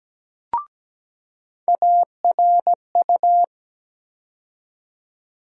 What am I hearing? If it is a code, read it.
Morse code, heard as a keyed tone, about 17 wpm, 700 Hz. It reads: ARU
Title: USA Call Sign Postfixes - 3 Letters 17wpm